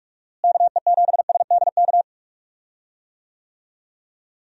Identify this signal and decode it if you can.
Morse code, heard as a keyed tone, about 43 wpm, 700 Hz. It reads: KE7SDK
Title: USA Call Signs - 6 Characters - Repeated 2x - 45wpm